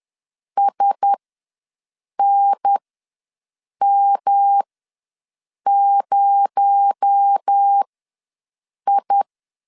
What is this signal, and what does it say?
Morse code, heard as a keyed tone, about 10 wpm, spaced as for 6 wpm, 800 Hz. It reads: SNM0I
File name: LEZIONI / 7 GF9Z / 7 GF9Z ws 10wpm 3919